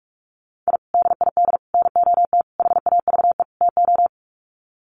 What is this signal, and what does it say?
Morse code, heard as a keyed tone, about 44 wpm, 700 Hz. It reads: I DID NOT HAVE TO
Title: Sets of 5 Words 45wpm